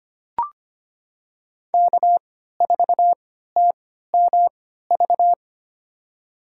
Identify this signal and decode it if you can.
Morse code, heard as a keyed tone, about 25 wpm, spaced as for 14 wpm, 700 Hz. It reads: K4TMV
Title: USA Call Signs - 5 Characters - 3x Character Spacing - 25wpm